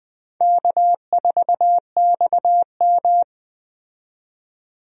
Morse code, heard as a keyed tone, about 20 wpm, 700 Hz. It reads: K4XM
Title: USA Call Signs - 4 Characters - 20wpm